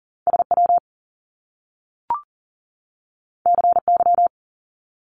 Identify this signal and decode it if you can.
Morse code, heard as a keyed tone, about 40 wpm, 700 Hz. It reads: SW CY
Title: Top Two Letter Combinations in Words 40wpm